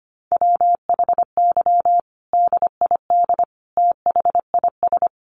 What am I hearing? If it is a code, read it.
Morse code, heard as a keyed tone, about 25 wpm, 700 Hz. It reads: WHY DID THIS
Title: Sentences from Top 900 Words 25wpm